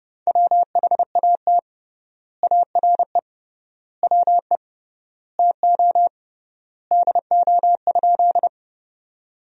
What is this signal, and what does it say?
Morse code, heard as a keyed tone, about 30 wpm, 700 Hz. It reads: WHAT ARE WE TO DO?
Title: 3x Word-Spacing - Sentences from Top 100 Words - 30wpm